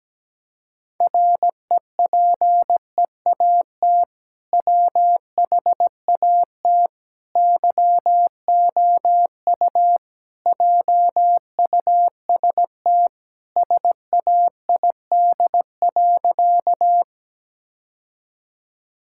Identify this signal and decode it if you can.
Morse code, heard as a keyed tone, about 17 wpm, 700 Hz. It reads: REPEAT WHAT YOU JUST SAID.
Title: Sentences from Top 1600 Words 17wpm